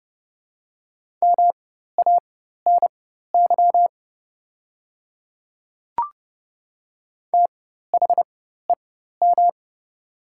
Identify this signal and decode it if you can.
Morse code, heard as a keyed tone, about 30 wpm, spaced as for 14 wpm, 700 Hz. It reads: MANY THEM
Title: 4x Character Spacing - Top 100 Words - 30wpm